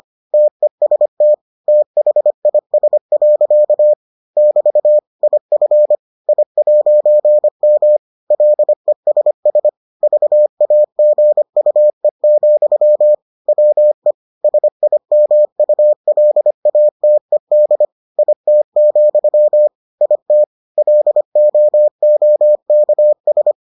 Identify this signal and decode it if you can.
Morse code, heard as a keyed tone, about 25 wpm, 600 Hz. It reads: TEST THIS. <BT> IF I'M LESS VAGUE, WE SIMULATED IT, IT LOOKS